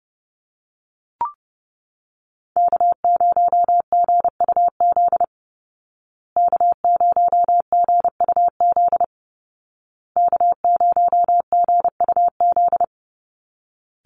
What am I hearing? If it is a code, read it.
Morse code, heard as a keyed tone, about 30 wpm, 700 Hz. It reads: K0GUZ K0GUZ K0GUZ
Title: Call Signs Repeated 2x - 30wpm